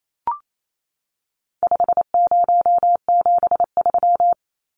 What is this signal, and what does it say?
Morse code, heard as a keyed tone, about 28 wpm, 700 Hz. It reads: 5073
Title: Numbers - 4-Digit 28wpm